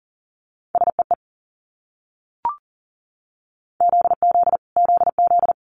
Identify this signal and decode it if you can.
Morse code, heard as a keyed tone, about 37 wpm, 700 Hz. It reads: SEE ZZ ZZ
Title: Multi-letter Phonemes Repeated 3x 40wpm